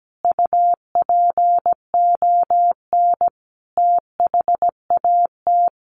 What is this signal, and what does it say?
Morse code, heard as a keyed tone, about 17 wpm, 700 Hz. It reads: UPON THAT
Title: Sentences from Top 1500 Words 17wpm